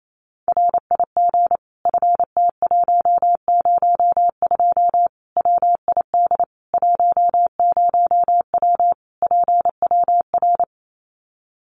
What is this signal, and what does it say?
Morse code, heard as a keyed tone, about 28 wpm, 700 Hz. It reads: RIG FT102 WID 10W PWR
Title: Common QSO Phrases 28wpm